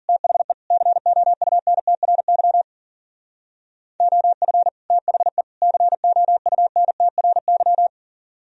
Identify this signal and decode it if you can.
Morse code, heard as a keyed tone, about 51 wpm, 700 Hz. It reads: THE COUNTRY OF THE COUNTRY